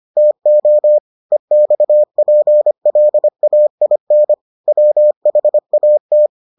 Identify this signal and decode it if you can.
Morse code, heard as a keyed tone, about 25 wpm, 600 Hz. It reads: TO EXPLAIN WHAT